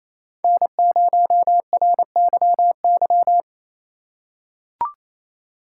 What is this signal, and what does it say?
Morse code, heard as a keyed tone, about 28 wpm, 700 Hz. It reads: N0RYY
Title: USA Call Signs - 5 Characters - 6x Character Spacing - 28wpm